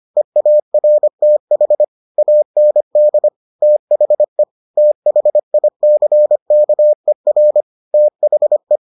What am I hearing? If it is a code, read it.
Morse code, heard as a keyed tone, about 25 wpm, 600 Hz. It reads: EARTH AND THE THICKER THE